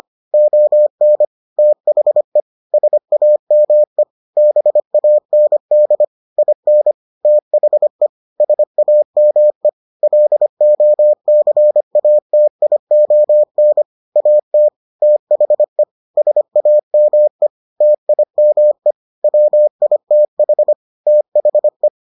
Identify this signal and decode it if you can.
Morse code, heard as a keyed tone, about 25 wpm, 600 Hz. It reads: ON THE SAME BAND IN THE SAME LOCATION AT THE SAME TIME WITH THE